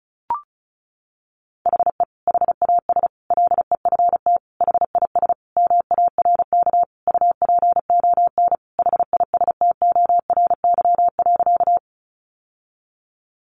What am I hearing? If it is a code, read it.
Morse code, heard as a keyed tone, about 34 wpm, 700 Hz. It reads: HE HAS LEFT HIS MARK UPON HISTORY.
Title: Sentences from Top 1100 Words 35wpm